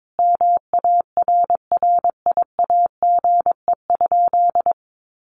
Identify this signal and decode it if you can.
Morse code, heard as a keyed tone, about 22 wpm, 700 Hz. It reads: MARRIAGE?
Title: Sentences from Top 1200 Words 22wpm